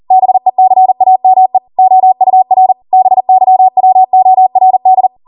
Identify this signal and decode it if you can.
Morse code, heard as a keyed tone, about 40 wpm, 750 Hz. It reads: BECAME OUR BYWORD